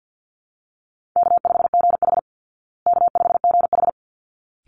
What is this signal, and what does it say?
Morse code, heard as a keyed tone, about 50 wpm, 700 Hz. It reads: K5GH K5GH
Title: Easy Call Signs Repeated 2x - 50wpm